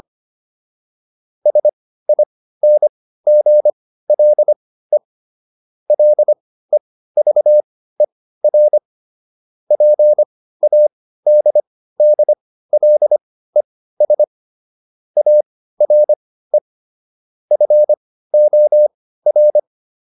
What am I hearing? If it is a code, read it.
Morse code, heard as a keyed tone, about 25 wpm, spaced as for 15 wpm, 600 Hz. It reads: SINGLE LEVER PADDLES ARE FOR